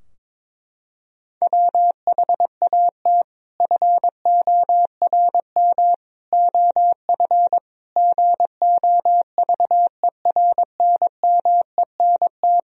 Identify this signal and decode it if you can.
Morse code, heard as a keyed tone, about 22 wpm, 700 Hz. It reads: WHAT FORM OF GOVERNMENT
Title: Sentences from Top 1000 Words - Review Part 3 - 22wpm